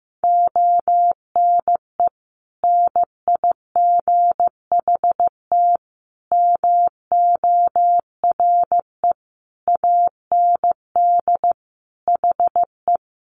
Morse code, heard as a keyed tone, about 15 wpm, 700 Hz. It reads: ONE NIGHT MORE AND HE
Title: Sentences from Top 300 Words 15wpm